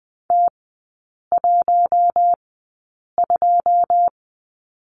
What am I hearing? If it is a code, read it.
Morse code, heard as a keyed tone, about 20 wpm, spaced as for 8 wpm, 700 Hz. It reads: T12
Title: 114 Lesson 19e - Letter w - Three Characters 20wpm